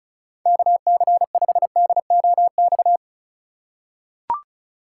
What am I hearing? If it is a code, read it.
Morse code, heard as a keyed tone, about 36 wpm, 700 Hz. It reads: KC5DOX
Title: USA Call Signs - 6 Characters - 3x Character Spacing - 35wpm